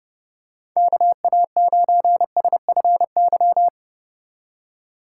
Morse code, heard as a keyed tone, about 30 wpm, 700 Hz. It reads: KA9SFY